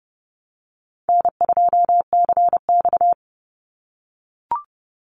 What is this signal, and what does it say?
Morse code, heard as a keyed tone, about 30 wpm, 700 Hz. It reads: N2CX
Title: Call Signs - 30wpm